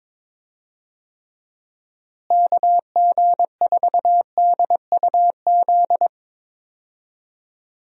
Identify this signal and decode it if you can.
Morse code, heard as a keyed tone, about 22 wpm, 700 Hz. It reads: KG4DUZ
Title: USA Call Signs - 6 Characters - Repeated 3x - 22wpm